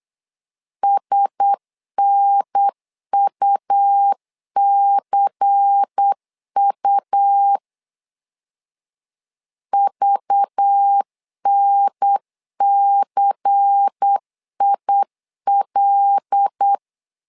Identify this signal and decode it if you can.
Morse code, heard as a keyed tone, about 8 wpm, 800 Hz. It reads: SNUCU VNCIL
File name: LEZIONI / 5 LBVC / 5 LBVC 8wpm 2935